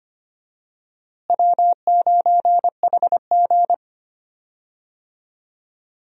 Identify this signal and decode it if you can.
Morse code, heard as a keyed tone, about 25 wpm, 700 Hz. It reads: W9HG
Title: USA Call Signs - 4 Characters - Repeated 3x - 25wpm